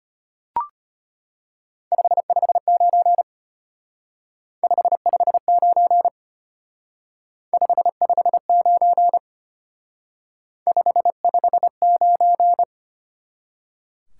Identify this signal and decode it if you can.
Morse code, heard as a keyed tone, about 32 wpm, 700 Hz. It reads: 559 559 559 559